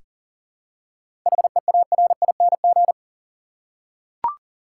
Morse code, heard as a keyed tone, about 40 wpm, 700 Hz. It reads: HEARING
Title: Seven Letter Words - 40wpm